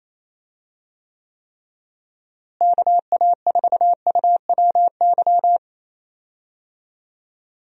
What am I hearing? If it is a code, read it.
Morse code, heard as a keyed tone, about 28 wpm, 700 Hz. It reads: KA4UWY
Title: USA Call Signs - 6 Characters - Repeated 3x - 28wpm